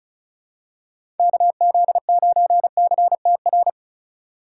Morse code, heard as a keyed tone, about 34 wpm, 700 Hz. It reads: KZ9CTR
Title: USA Call Signs - 4x Character Spacing - 35wpm